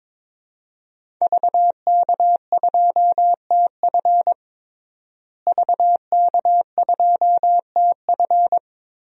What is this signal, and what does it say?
Morse code, heard as a keyed tone, about 22 wpm, 700 Hz. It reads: VK2TF VK2TF
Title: DXCC-100 Call Signs - 5 Characters - Repeated 3x - 22wpm